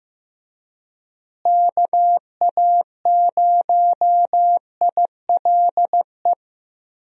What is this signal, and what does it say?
Morse code, heard as a keyed tone, about 15 wpm, 700 Hz. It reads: KA0ILE
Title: USA Call Signs - 6 Characters - Repeated 2x - 15wpm